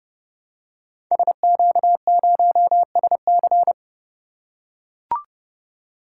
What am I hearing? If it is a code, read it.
Morse code, heard as a keyed tone, about 30 wpm, 700 Hz. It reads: SQ0SC
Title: Code Groups - 5 Letter:Numbers - 30wpm